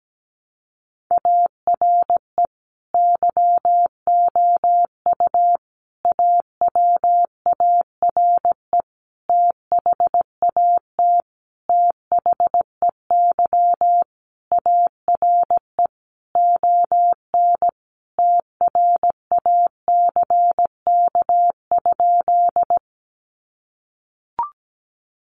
Morse code, heard as a keyed tone, about 17 wpm, 700 Hz. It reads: ARE YOU AWARE THAT THEY ARE ON TRACK?
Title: Sentences from Top 1200 Words 17wpm